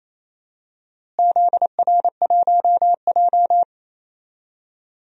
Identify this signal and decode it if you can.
Morse code, heard as a keyed tone, about 28 wpm, 700 Hz. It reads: ZR1J
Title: DXCC-100 Call Signs - 4 Characters - Repeated 2x - 28wpm